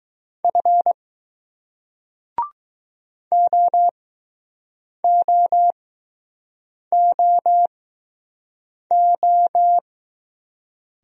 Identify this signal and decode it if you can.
Morse code, heard as a keyed tone, about 20 wpm, spaced as for 6 wpm, 700 Hz. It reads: F OOOO